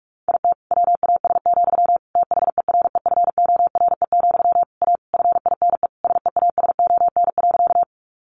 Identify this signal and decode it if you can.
Morse code, heard as a keyed tone, about 44 wpm, 700 Hz. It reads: IT WAS, THEREFORE, A FINE SEASON.
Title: Sentences from Top 1400 Words 45wpm